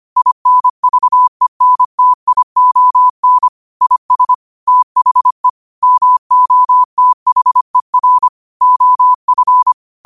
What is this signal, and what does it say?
Morse code, heard as a keyed tone, about 25 wpm, 1000 Hz. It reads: INVENTION IS THE MOTHER OF
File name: qotdfiles / 25wpm